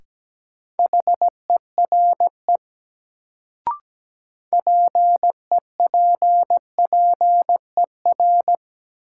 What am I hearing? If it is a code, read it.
Morse code, heard as a keyed tone, about 17 wpm, 700 Hz. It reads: HERE PEPPER
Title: Sets of 5 Words - New Word Drill - 17wpm